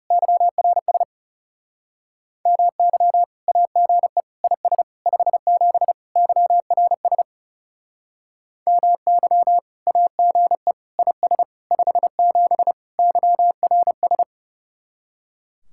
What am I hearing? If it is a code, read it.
Morse code, heard as a keyed tone, about 38 wpm, 700 Hz. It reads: YRS MY AGE IS 57 YRS MY AGE IS 57 YRS